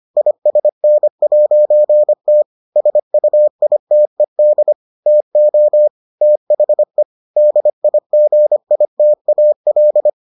Morse code, heard as a keyed tone, about 25 wpm, 600 Hz. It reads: ISN'T SUITED TO THE DIGITAL